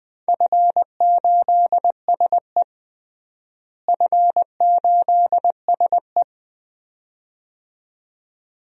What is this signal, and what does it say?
Morse code, heard as a keyed tone, about 20 wpm, 700 Hz. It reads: F8SE F8SE
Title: DXCC-100 Call Signs - 4 Characters - Repeated 2x - 20wpm